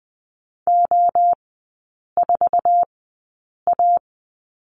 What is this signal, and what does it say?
Morse code, heard as a keyed tone, about 20 wpm, spaced as for 8 wpm, 700 Hz. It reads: O4A
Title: 047 Lesson 9e - Number 4 - Three Characters 20wpm